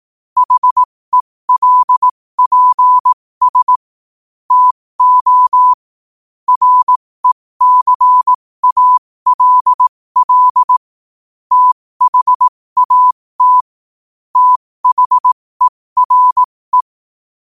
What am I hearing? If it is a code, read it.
Morse code, heard as a keyed tone, about 18 wpm, spaced as for 15 wpm, 1000 Hz. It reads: HELPS TO RECALL THAT THERE
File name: qotdfiles / 15wpm